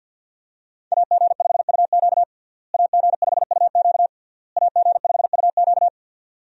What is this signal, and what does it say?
Morse code, heard as a keyed tone, about 52 wpm, 700 Hz. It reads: AG5UQ AG5UQ AG5UQ
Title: USA Call Signs - 5 Characters - Repeated 3x - 50wpm